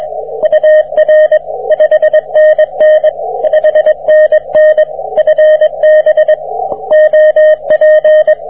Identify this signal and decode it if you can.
Morse code, heard as a keyed tone, about 22 wpm, 620 Hz, with radio noise. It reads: UR 5NN 5NN FB OP